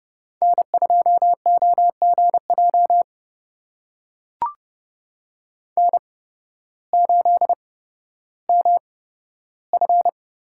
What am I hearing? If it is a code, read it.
Morse code, heard as a keyed tone, about 30 wpm, spaced as for 14 wpm, 700 Hz. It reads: N2OGJ N8MF
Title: USA Call Signs - 5 Characters - 8x Character Spacing - 30wpm